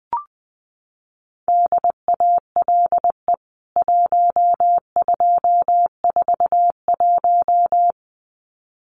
Morse code, heard as a keyed tone, about 19 wpm, 700 Hz. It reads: DALE 1241
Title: CWT Exchange 20wpm